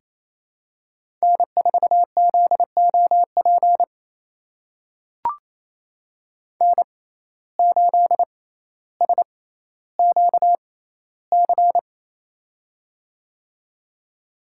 Morse code, heard as a keyed tone, about 28 wpm, spaced as for 14 wpm, 700 Hz. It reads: N4ZOP N8SQC
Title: USA Call Signs - 5 Characters - 6x Character Spacing - 28wpm